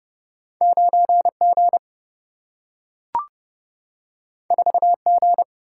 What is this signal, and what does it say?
Morse code, heard as a keyed tone, about 30 wpm, 700 Hz. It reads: 9 G 4 G
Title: Single-Letter Single-Number 30wpm